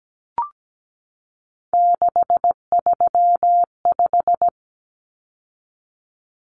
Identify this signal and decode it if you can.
Morse code, heard as a keyed tone, about 17 wpm, 700 Hz. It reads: 635
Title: Common Cut Numbers - 3-Digit 17wpm